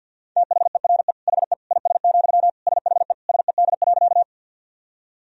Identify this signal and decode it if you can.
Morse code, heard as a keyed tone, about 50 wpm, 700 Hz. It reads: THERE HE IS, SHE SED.
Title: Sentences from Top 100 Words and Abbreviations 50wpm